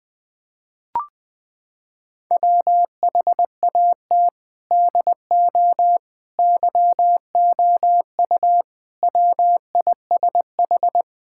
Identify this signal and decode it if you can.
Morse code, heard as a keyed tone, about 21 wpm, 700 Hz. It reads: WHAT DO YOU WISH
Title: Sentences from Top 1600 Words 20wpm